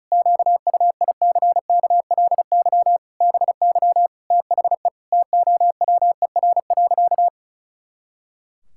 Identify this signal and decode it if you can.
Morse code, heard as a keyed tone, about 36 wpm, 700 Hz. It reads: QUICKLY BY THE TOWER.
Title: Pangram Challenge v2 - 35wpm